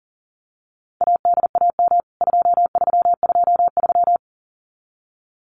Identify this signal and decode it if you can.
Morse code, heard as a keyed tone, about 40 wpm, 700 Hz. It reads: ADAM 2323
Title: CWT Exchange 40wpm